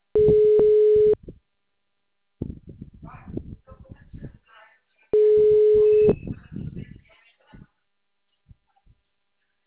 Ringback tone (Europe, 425 Hz).
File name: out- -2014- - - .50